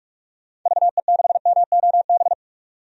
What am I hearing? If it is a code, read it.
Morse code, heard as a keyed tone, about 41 wpm, 700 Hz. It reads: VE6MOB
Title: DXCC-100 Call Signs - 6 Characters - 45wpm